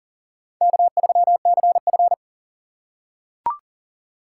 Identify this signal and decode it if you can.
Morse code, heard as a keyed tone, about 40 wpm, 700 Hz. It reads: K3CF
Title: USA Call Signs - 4 Characters - 3x Character Spacing - 40wpm